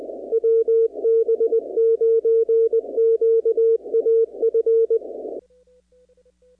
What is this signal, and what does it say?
Morse code, heard as a keyed tone, about 20 wpm, 450 Hz. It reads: WB9QAF
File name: contests / 2019-cw-wpx-kt5j-k5pi- -2330 0-44